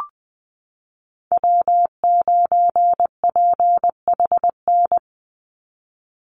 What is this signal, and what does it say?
Morse code, heard as a keyed tone, about 20 wpm, 700 Hz. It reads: W9PHN
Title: USA Call Signs 20wpm